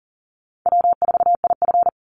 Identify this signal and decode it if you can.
Morse code, heard as a keyed tone, about 40 wpm, 700 Hz. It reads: W4IF
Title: Call Signs - 40wpm